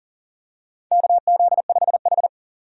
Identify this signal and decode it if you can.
Morse code, heard as a keyed tone, about 40 wpm, 700 Hz. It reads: KZ5H
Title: USA Call Signs - 4 Characters - 40wpm